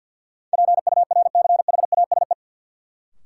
Morse code, heard as a keyed tone, about 48 wpm, 700 Hz. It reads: PURCHASE
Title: Sentences from Top 800 Words - New Word Drill - 50wpm